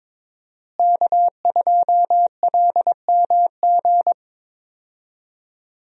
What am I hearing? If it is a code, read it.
Morse code, heard as a keyed tone, about 22 wpm, 700 Hz. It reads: K2LMG
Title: USA Call Signs - 5 Characters - Repeated 3x - 22wpm